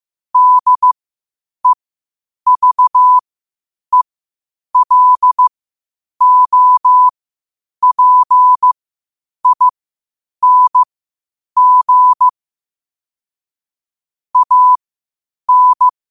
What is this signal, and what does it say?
Morse code, heard as a keyed tone, about 15 wpm, spaced as for 8 wpm, 1000 Hz. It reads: DEVELOPING AN
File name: qotdfiles / 8wpm